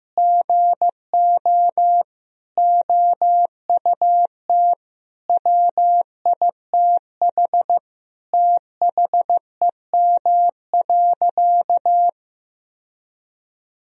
Morse code, heard as a keyed tone, about 16 wpm, 700 Hz. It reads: GO OUT WITH THEM.
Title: Sentences from Top 1000 Words - Review Part 3 - 15wpm